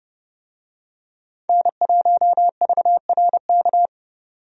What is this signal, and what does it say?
Morse code, heard as a keyed tone, about 30 wpm, 700 Hz. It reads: N1VRK
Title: USA Call Signs - 5 Characters - 30wpm